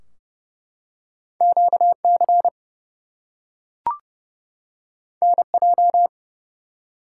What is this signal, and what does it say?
Morse code, heard as a keyed tone, about 30 wpm, 700 Hz. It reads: QC NJ
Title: US States and Canadian Provinces 30wpm